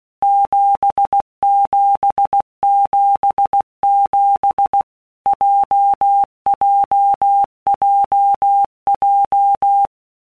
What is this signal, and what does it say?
Morse code, heard as a keyed tone, about 16 wpm, 800 Hz. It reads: 7777 JJJJ
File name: Basic Letter Groups By Blocks / Groups@15 / JZ17@15